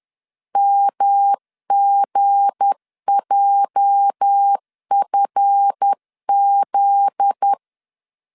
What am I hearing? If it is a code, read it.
Morse code, heard as a keyed tone, about 10 wpm, 800 Hz. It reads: MGJFZ